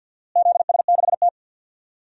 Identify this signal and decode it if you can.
Morse code, heard as a keyed tone, about 52 wpm, 700 Hz. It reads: ZS6T
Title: DXCC-100 Call Signs - 4 Characters - Repeated 3x - 50wpm